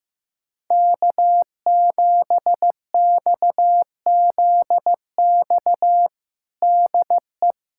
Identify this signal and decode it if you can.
Morse code, heard as a keyed tone, about 15 wpm, 700 Hz. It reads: K7XZX DE
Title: USA Callsign DE Callsign 15wpm